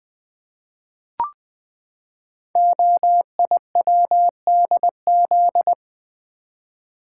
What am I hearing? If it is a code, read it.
Morse code, heard as a keyed tone, about 20 wpm, 700 Hz. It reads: OIWDZ
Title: Code Groups - 5 Letters - 20wpm